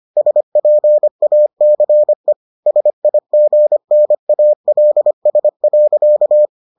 Morse code, heard as a keyed tone, about 25 wpm, 600 Hz. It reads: SPACE SIGNALS.